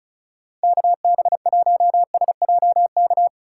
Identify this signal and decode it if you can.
Morse code, heard as a keyed tone, about 36 wpm, 700 Hz. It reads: KB1SJK
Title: USA Call Signs - 6 Characters - 3x Character Spacing - 35wpm